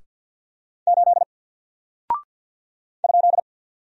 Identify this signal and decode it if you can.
Morse code, heard as a keyed tone, about 48 wpm, 700 Hz. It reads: 8 ?
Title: Single Character (Letter:Number:Punctuation:Prosign) 50wpm